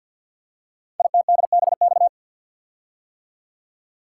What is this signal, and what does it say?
Morse code, heard as a keyed tone, about 50 wpm, 700 Hz. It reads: ITDBX
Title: Code Groups - 5 Letters - 50wpm